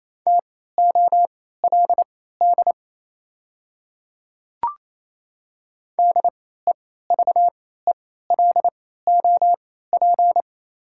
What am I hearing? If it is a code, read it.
Morse code, heard as a keyed tone, about 28 wpm, spaced as for 16 wpm, 700 Hz. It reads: TOLD DEVELOP
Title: 3x Character Spacing - Top 500 Words - 28wpm